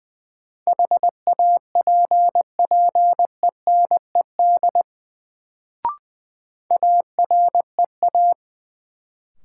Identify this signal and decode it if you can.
Morse code, heard as a keyed tone, about 20 wpm, 700 Hz. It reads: HAPPENED AREA
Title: Sentences from Top 1000 Words - Review Part 2 - New Word Drill - 20wpm